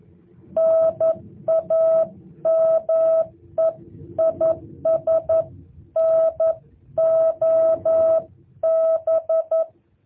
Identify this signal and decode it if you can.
Morse code, heard as a keyed tone, about 11 wpm, 650 Hz. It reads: NAME IS NOB
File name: jh8cbh / 2023cw